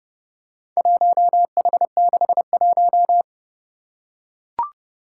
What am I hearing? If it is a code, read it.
Morse code, heard as a keyed tone, about 30 wpm, 700 Hz. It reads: 1H61
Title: Four Challenging Letter-Number Combinations 30wpm